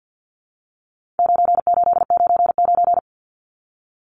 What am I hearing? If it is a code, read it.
Morse code, heard as a keyed tone, about 49 wpm, 700 Hz. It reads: 9899
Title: Numbers - 4-Digit 50wpm